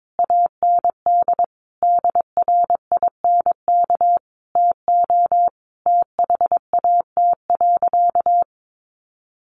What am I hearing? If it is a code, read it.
Morse code, heard as a keyed tone, about 22 wpm, 700 Hz. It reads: AND DRINK TO THAT.
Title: Sentences from Top 1500 Words 22wpm